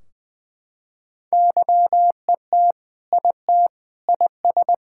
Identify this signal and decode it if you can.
Morse code, heard as a keyed tone, about 20 wpm, 700 Hz. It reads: YET IT IS
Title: Sentences from Top 1600 Words 20wpm